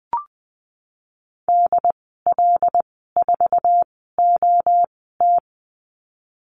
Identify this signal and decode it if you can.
Morse code, heard as a keyed tone, about 20 wpm, spaced as for 15 wpm, 700 Hz. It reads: DL4OT
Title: 082 Lesson 14g - Number 2 - Call Signs 20wpm